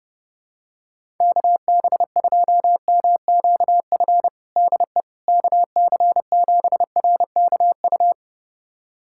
Morse code, heard as a keyed tone, about 30 wpm, 700 Hz. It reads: KB2MQF DE KC7RKU
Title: USA Callsign DE Callsign 30wpm